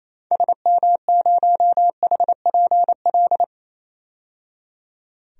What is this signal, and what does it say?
Morse code, heard as a keyed tone, about 28 wpm, 700 Hz. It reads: SM0HPL
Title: DXCC-100 Call Signs - 6 Characters - 28wpm